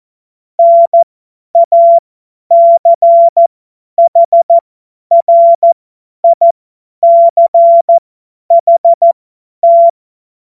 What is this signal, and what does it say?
Morse code, heard as a keyed tone, about 14 wpm, spaced as for 10 wpm, 680 Hz. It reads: NACHRICHT
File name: Morsekurs / Witze / Witz 12 März 2024 50 F70